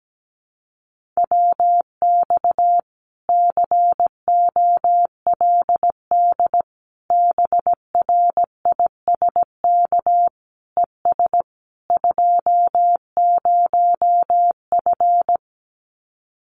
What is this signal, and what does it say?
Morse code, heard as a keyed tone, about 17 wpm, 700 Hz. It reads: WX COLD BRISK ES 20F
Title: Common QSO Phrases Encore 17wpm